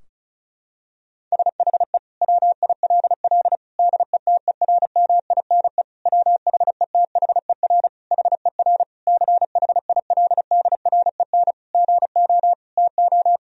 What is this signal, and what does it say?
Morse code, heard as a keyed tone, about 34 wpm, 700 Hz. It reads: SHE WILL DETERMINE WHETHER HER CHILDREN GO TO
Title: Sentences from Top 1000 Words - Review Part 4 - 35wpm